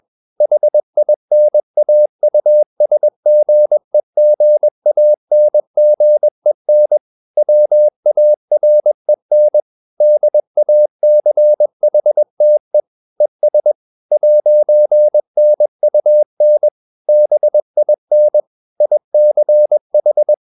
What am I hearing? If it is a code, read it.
Morse code, heard as a keyed tone, about 21 wpm, 600 Hz. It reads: HINAUSGEGANGEN WAREN DACHTE ES 'NUN BIN ICH